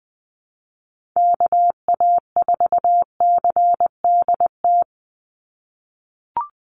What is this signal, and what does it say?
Morse code, heard as a keyed tone, about 20 wpm, 700 Hz. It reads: KA4CDT
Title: USA Call Signs 20wpm